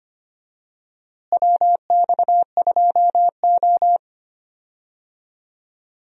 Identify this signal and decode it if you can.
Morse code, heard as a keyed tone, about 25 wpm, 700 Hz. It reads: WX2O
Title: USA Call Signs - 4 Characters - Repeated 2x - 25wpm